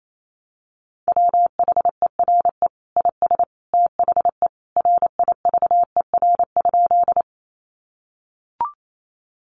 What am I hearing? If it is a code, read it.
Morse code, heard as a keyed tone, about 28 wpm, 700 Hz. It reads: WHERE IS THE RIVER?
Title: Sentences from Top 300 Words 28wpm